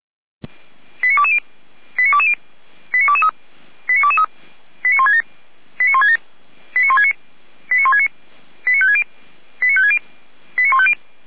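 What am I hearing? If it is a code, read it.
Five-tone selective call (ZVEI), heard as quick runs of five tones, tones of about 70 ms. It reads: 89220 89220 89223 89223 89167 89167 89169 89169 89560 89560 89150
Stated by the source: ZVEI five-tone selective call received on an SDR, from fire brigade Feuerwehr Orlamünde.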